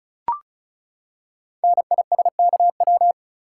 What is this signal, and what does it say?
Morse code, heard as a keyed tone, about 36 wpm, 690 Hz. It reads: NISKW
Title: Code Groups - 5 Letters - 35wpm